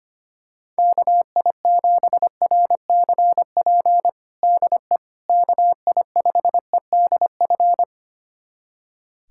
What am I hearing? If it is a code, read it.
Morse code, heard as a keyed tone, about 25 wpm, 700 Hz. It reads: KI7RCP DE KI5EDF
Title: USA Callsign DE Callsign 25wpm